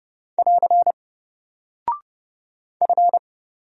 Morse code, heard as a keyed tone, about 30 wpm, 700 Hz. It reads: <AR> F
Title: Single Character (Letter:Number:Punctuation:Prosign) Alt 30wpm